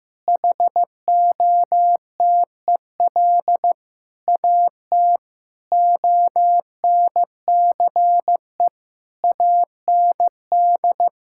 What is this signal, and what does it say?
Morse code, heard as a keyed tone, about 15 wpm, 700 Hz. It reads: HOTEL AT ONCE AND
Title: Sentences from Top 1600 Words 15wpm